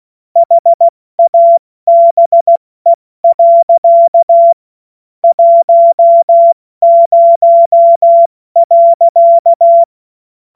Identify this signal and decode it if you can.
Morse code, heard as a keyed tone, about 16 wpm, 680 Hz. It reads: HABE. 10.
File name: Morsekurs / Witze / Witz 06. September 2022 70 F80